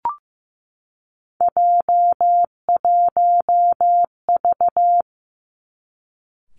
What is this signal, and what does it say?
Morse code, heard as a keyed tone, about 15 wpm, 700 Hz. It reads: J1V
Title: Three Challenging Letter-Number Combinations 15wpm